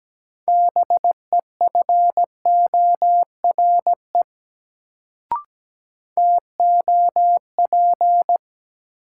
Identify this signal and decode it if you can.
Morse code, heard as a keyed tone, about 17 wpm, 700 Hz. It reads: BEFORE TOP
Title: Sets of 2 Words - New Word Drill - 17wpm